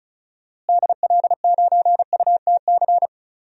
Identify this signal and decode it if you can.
Morse code, heard as a keyed tone, about 34 wpm, 700 Hz. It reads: DL9UTC
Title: DXCC-100 Call Signs - 6 Characters - Repeated 3x - 35wpm